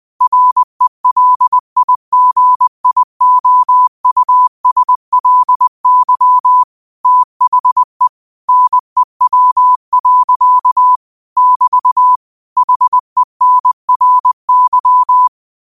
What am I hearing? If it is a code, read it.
Morse code, heard as a keyed tone, about 20 wpm, 1000 Hz. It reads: RELIGIOUSLY THE NEW. <BT> HENRY
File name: qotdfiles / 20wpm